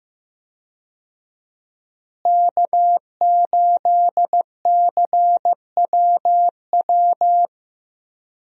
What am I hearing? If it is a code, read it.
Morse code, heard as a keyed tone, about 15 wpm, 700 Hz. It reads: K8CWW
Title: USA Call Signs - 5 Characters - Repeated 2x - 15wpm